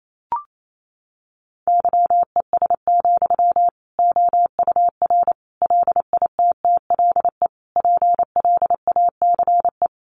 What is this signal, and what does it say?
Morse code, heard as a keyed tone, about 28 wpm, 700 Hz. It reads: YES, OUR LITTLE PLACE
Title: Sentences from Top 700 Words 28wpm